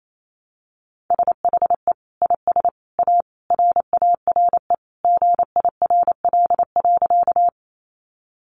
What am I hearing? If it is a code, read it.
Morse code, heard as a keyed tone, about 28 wpm, 700 Hz. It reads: SHE IS A RARE GIRL.
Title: Sentences from Top 1100 Words 28wpm